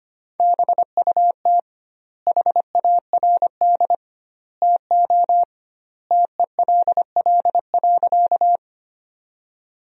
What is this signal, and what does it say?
Morse code, heard as a keyed tone, about 25 wpm, 700 Hz. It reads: BUT HARD TO TELL.
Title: 2x Word-Spacing - Sentences from Top 300 Words - 25wpm